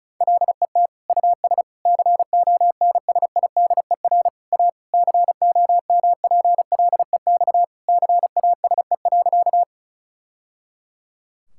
Morse code, heard as a keyed tone, about 36 wpm, 700 Hz. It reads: LET US CONSIDER A COMPLEX CASE.
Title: Sentences from Top 1000 Words - Review Part 6 - 35wpm